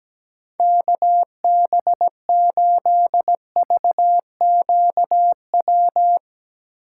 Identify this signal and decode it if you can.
Morse code, heard as a keyed tone, about 17 wpm, 700 Hz. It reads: KB8VQW
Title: USA Call Signs - 6 Characters - 3x Character Spacing - 17wpm